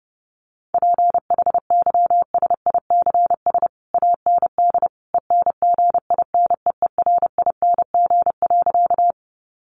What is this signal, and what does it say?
Morse code, heard as a keyed tone, about 30 wpm, 700 Hz. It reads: PHYSICS AND ENGINEERING.
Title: Sentences from Top 1500 Words 30wpm